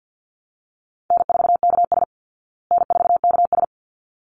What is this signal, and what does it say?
Morse code, heard as a keyed tone, about 50 wpm, 700 Hz. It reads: N4KS N4KS
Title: Easy Call Signs Repeated 2x - 50wpm